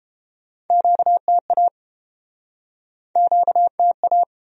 Morse code, heard as a keyed tone, about 30 wpm, 700 Hz. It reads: QTA QTA